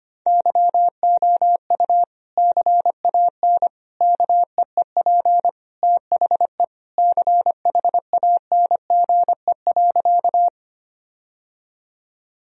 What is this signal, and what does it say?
Morse code, heard as a keyed tone, about 25 wpm, 700 Hz. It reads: YOU CAN KEEP THE CHANGE.
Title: Sentences from Top 1000 Words - Review Part 1 - 25wpm